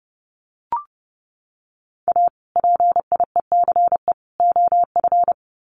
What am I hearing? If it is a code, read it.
Morse code, heard as a keyed tone, about 30 wpm, 700 Hz. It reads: A PIECE OF
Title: Sets of 3 Words 30wpm